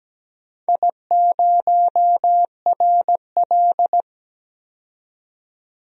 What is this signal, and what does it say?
Morse code, heard as a keyed tone, about 17 wpm, 700 Hz. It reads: I0RL
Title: DXCC-100 Call Signs - 4 Characters - 17wpm